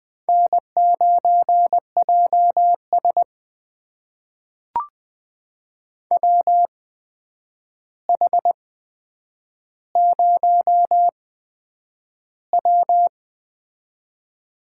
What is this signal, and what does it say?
Morse code, heard as a keyed tone, about 21 wpm, spaced as for 9 wpm, 700 Hz. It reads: N9JS WH0W
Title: USA Call Signs - 4 Characters - 8x Character Spacing - 20wpm